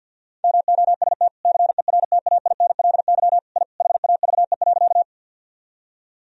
Morse code, heard as a keyed tone, about 52 wpm, 700 Hz. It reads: MOST CERTAINLY I HAVE.
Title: Sentences from Top 1000 Words - Review Part 1 - 50wpm